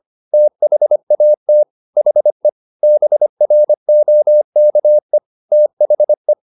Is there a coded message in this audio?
Yes: Morse code from 0.3 to 6.3 s, heard as a keyed tone, about 25 wpm, 600 Hz. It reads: THAT HE BROKE THE